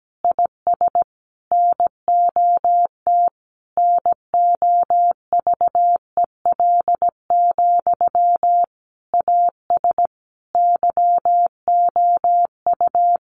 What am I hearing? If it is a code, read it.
Morse code, heard as a keyed tone, about 17 wpm, 700 Hz. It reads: IS NOT NOVEL, AS YOU
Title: Sentences from Top 1300 Words 17wpm